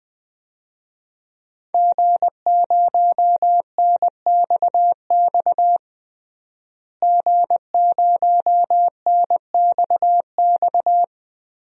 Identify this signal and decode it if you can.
Morse code, heard as a keyed tone, about 21 wpm, 700 Hz. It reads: G0NXX G0NXX
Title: DXCC-100 Call Signs - 5 Characters - Repeated 2x - 20wpm